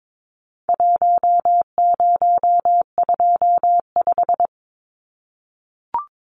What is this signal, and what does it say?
Morse code, heard as a keyed tone, about 22 wpm, 700 Hz. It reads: 1025
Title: Numbers - Contest Style - 4-Digit 22wpm